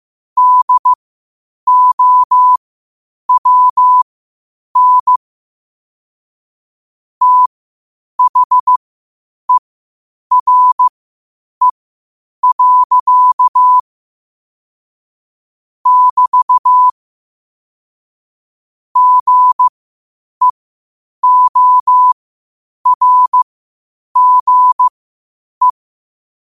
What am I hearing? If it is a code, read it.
Morse code, heard as a keyed tone, about 15 wpm, spaced as for 8 wpm, 1000 Hz. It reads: DOWN THERE. <BT> GEORGE
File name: qotdfiles / 8wpm